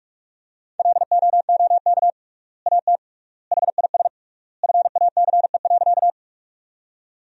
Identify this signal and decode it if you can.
Morse code, heard as a keyed tone, about 44 wpm, 700 Hz. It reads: LOOK AT HIS FACE.
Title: 3x Word-Spacing - Sentences from Top 300 Words - 45wpm